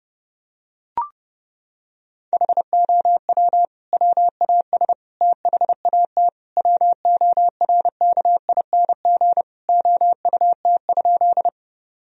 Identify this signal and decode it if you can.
Morse code, heard as a keyed tone, about 32 wpm, 700 Hz. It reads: HOW WAS THAT WORKING OUT?
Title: Sentences from Top 1000 Words - Review Part 4 - 30wpm